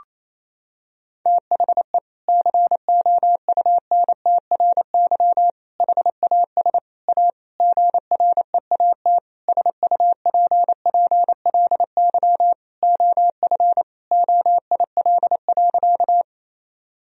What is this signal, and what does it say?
Morse code, heard as a keyed tone, about 28 wpm, 700 Hz. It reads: THE COUNTRY HAS A GREAT SUPPLY OF OIL.
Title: Sentences from Top 1000 Words - Review Part 4 - 28wpm